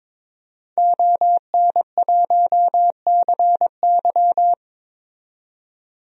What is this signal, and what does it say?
Morse code, heard as a keyed tone, about 22 wpm, 700 Hz. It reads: ON1CY
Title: DXCC-100 Call Signs - 5 Characters - 22wpm